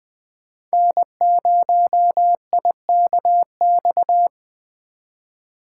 Morse code, heard as a keyed tone, about 21 wpm, 700 Hz. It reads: N0IKX